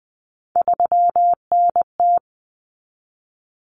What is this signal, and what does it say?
Morse code, heard as a keyed tone, about 20 wpm, 700 Hz. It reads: 3NT
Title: Common Cut Numbers - 3-Digit 20wpm